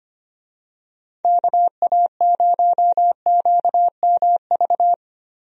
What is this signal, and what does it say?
Morse code, heard as a keyed tone, about 25 wpm, 700 Hz. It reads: KA0QMV
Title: USA Call Signs - 6 Characters - Repeated 3x - 25wpm